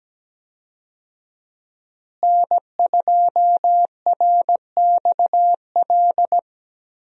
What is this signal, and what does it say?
Morse code, heard as a keyed tone, about 17 wpm, 700 Hz. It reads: N2RXL
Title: USA Call Signs - 5 Characters - Repeated 2x - 17wpm